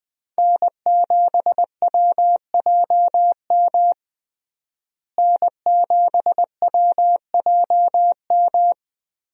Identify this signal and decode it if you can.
Morse code, heard as a keyed tone, about 21 wpm, 700 Hz. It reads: N7WJM N7WJM
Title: USA Call Signs - 5 Characters - Repeated 3x - 20wpm